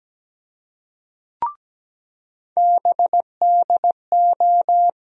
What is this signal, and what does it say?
Morse code, heard as a keyed tone, about 17 wpm, 700 Hz. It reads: BDO
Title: USA Call Sign Postfixes - 3 Letters 17wpm